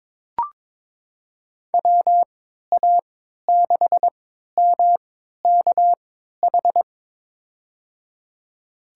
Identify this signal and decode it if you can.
Morse code, heard as a keyed tone, about 22 wpm, spaced as for 13 wpm, 700 Hz. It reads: WA6MKH
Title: USA Call Signs - 6 Characters - 3x Character Spacing - 22wpm